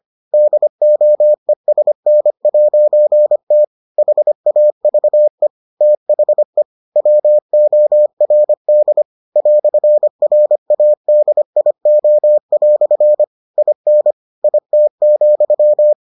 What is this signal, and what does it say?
Morse code, heard as a keyed tone, about 25 wpm, 600 Hz. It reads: DOESN'T HAVE THE WORD "RADIO" IN IT,